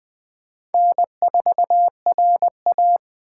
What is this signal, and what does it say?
Morse code, heard as a keyed tone, about 21 wpm, 700 Hz. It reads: N4RA
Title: Speed Racing - USA 1x2 Call Signs 17wpm